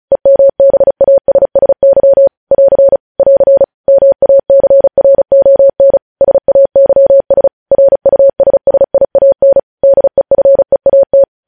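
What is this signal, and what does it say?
Morse code, heard as a keyed tone, about 36 wpm, 550 Hz. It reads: EMBASSY <AR> <AR> MACRON SAYS RUSSIAN DEFEAT
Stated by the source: Morse code practice reading news headlines at 35 words per minute.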